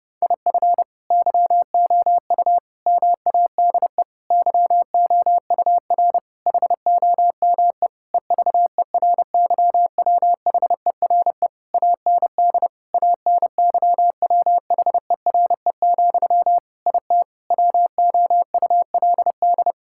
Morse code, heard as a keyed tone, about 30 wpm, 700 Hz. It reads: IF YOU MADE YOUR HOME EVERYWHERE AND ANYWHERE, IT WOULD
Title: Sentences from Top 1600 Words 30wpm